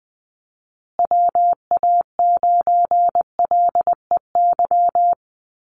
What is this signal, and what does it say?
Morse code, heard as a keyed tone, about 20 wpm, 700 Hz. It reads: WA9LEY
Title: Call Signs Repeated 2x - 20wpm